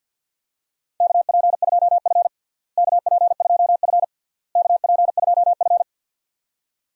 Morse code, heard as a keyed tone, about 47 wpm, 700 Hz. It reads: KP2F KP2F KP2F
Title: DXCC-100 Call Signs - 4 Characters - Repeated 3x - 50wpm